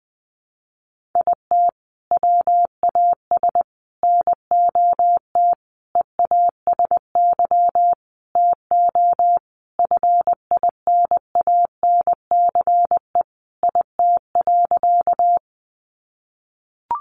Morse code, heard as a keyed tone, about 19 wpm, 700 Hz. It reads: IT WAS NOT EASY TO FINANCE IT.
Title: Sentences from Top 1500 Words 20wpm